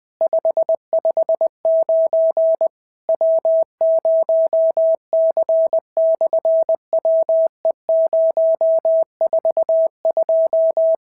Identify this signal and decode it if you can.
Morse code, heard as a keyed tone, about 20 wpm, 650 Hz. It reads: 559 W0C/WE042
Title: SOTA QSO Practice 20wpm